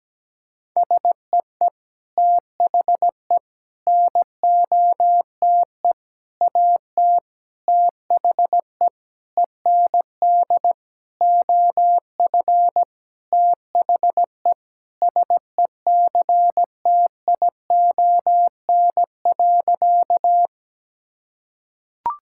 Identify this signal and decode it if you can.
Morse code, heard as a keyed tone, about 17 wpm, 700 Hz. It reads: SEE THE NOTE AT THE END OF THE SECTION.
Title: Sentences from Top 1000 Words - Review Part 5 - 17wpm